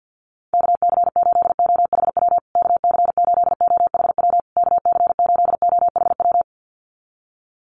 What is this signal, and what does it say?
Morse code, heard as a keyed tone, about 48 wpm, 700 Hz. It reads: KC8OHW KC8OHW KC8OHW
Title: USA Call Signs Repeated 3x 50wpm